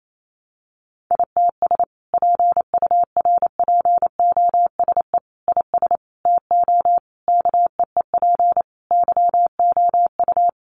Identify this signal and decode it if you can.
Morse code, heard as a keyed tone, about 28 wpm, 700 Hz. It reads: ITS PURPOSE IS TO KEEP YOU
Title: A Sound Of Thunder 28wpm